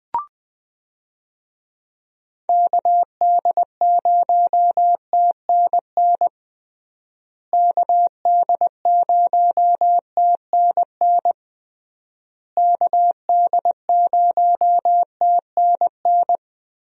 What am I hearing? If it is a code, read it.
Morse code, heard as a keyed tone, about 20 wpm, 700 Hz. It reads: KD0TNN KD0TNN KD0TNN
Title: USA Call Signs - 6 Characters - Repeated 3x - 20wpm